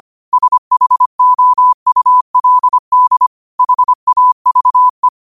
Morse code, heard as a keyed tone, about 25 wpm, 1000 Hz. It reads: SHOULD HAVE
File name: qotdfiles / 25wpm